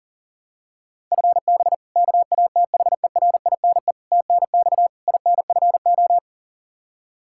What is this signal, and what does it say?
Morse code, heard as a keyed tone, about 40 wpm, 700 Hz. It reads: FB KATHERINE TNX INFO
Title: K1USN SST QSO Practice 40wpm